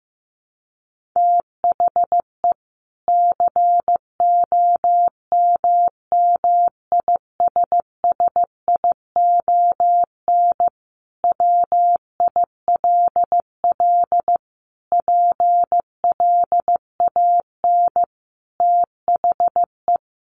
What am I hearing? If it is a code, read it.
Morse code, heard as a keyed tone, about 15 wpm, 700 Hz. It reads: THE COMMISSION WILL PLAN THE
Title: Sentences from Top 1400 Words 15wpm